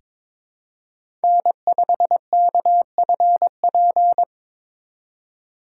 Morse code, heard as a keyed tone, about 23 wpm, 700 Hz. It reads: N5KFP
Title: USA Call Signs - 5 Characters - 4x Character Spacing - 22wpm